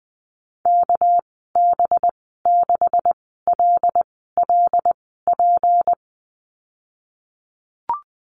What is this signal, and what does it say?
Morse code, heard as a keyed tone, about 20 wpm, spaced as for 15 wpm, 700 Hz. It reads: KB6LLP
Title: 200 Lesson 32g - Letter k - Call Signs 20wpm